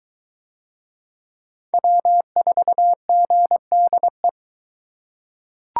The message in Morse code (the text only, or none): W4GDE